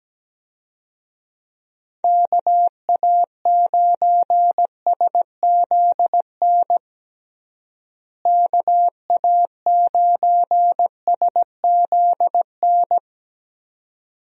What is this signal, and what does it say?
Morse code, heard as a keyed tone, about 17 wpm, 700 Hz. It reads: KA9SZN KA9SZN
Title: USA Call Signs - 6 Characters - Repeated 2x - 17wpm